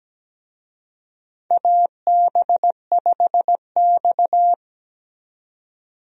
Morse code, heard as a keyed tone, about 17 wpm, 700 Hz. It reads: AB5X
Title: USA Call Signs - 4 Characters - 17wpm